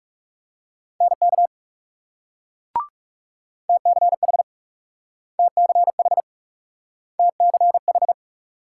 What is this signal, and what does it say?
Morse code, heard as a keyed tone, about 41 wpm, 700 Hz. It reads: NK TCH TCH TCH